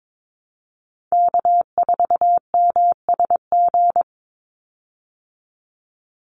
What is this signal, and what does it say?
Morse code, heard as a keyed tone, about 22 wpm, 700 Hz. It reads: K4MSG
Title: Call Signs Repeated 2x - 22wpm